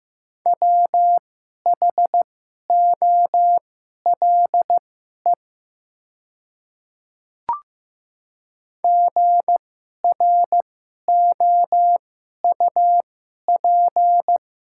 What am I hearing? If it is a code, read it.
Morse code, heard as a keyed tone, about 15 wpm, spaced as for 11 wpm, 700 Hz. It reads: WHOLE GROUP
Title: 2x Character Spacing - Top 500 Words - 15wpm